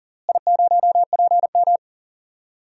Morse code, heard as a keyed tone, about 40 wpm, 700 Hz. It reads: I0PM
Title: DXCC-100 Call Signs - 4 Characters - Repeated 3x - 40wpm